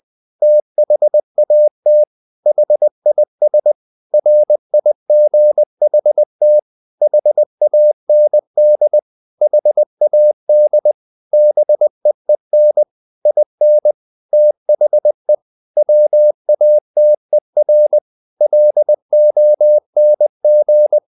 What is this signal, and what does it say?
Morse code, heard as a keyed tone, about 20 wpm, 600 Hz. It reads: THAT HIS RIGHT HAND HAD BEEN IN THE WATER LONG
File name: the old man and the sea part 2 20wpm